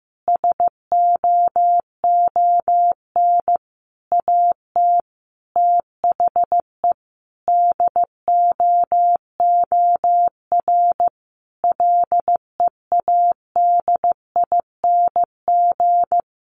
Morse code, heard as a keyed tone, about 15 wpm, 700 Hz. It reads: SOON AT THE DOOR LEADING
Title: Sentences from Top 1200 Words 15wpm